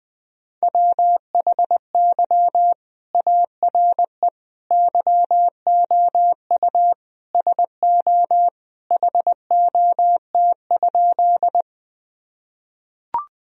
Morse code, heard as a keyed tone, about 19 wpm, 700 Hz. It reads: WHY ARE YOU SO HOT?
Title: Sentences from Top 1000 Words - Review Part 3 - 20wpm